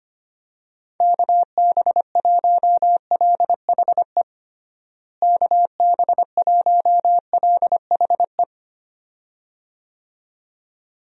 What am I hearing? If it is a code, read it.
Morse code, heard as a keyed tone, about 25 wpm, 700 Hz. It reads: KB1LHE KB1LHE
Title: USA Call Signs - 6 Characters - Repeated 2x - 25wpm